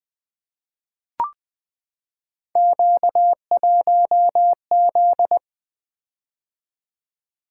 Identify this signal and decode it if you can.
Morse code, heard as a keyed tone, about 20 wpm, 700 Hz. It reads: Q1Z
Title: Three Letter-Number Combinations - 20wpm